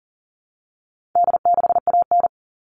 Morse code, heard as a keyed tone, about 40 wpm, 700 Hz. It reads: D6AN
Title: All Cut Numbers - 4-Digit 40wpm